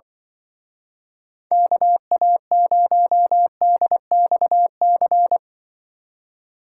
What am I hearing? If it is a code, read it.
Morse code, heard as a keyed tone, about 24 wpm, 700 Hz. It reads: KA0DXC
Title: Speed Racing - USA Call Signs 20wpm